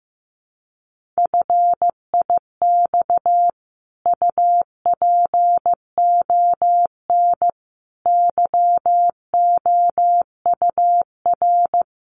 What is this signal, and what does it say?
Morse code, heard as a keyed tone, about 15 wpm, 700 Hz. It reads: FIX UPON YOUR
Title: Sentences from Top 1400 Words 15wpm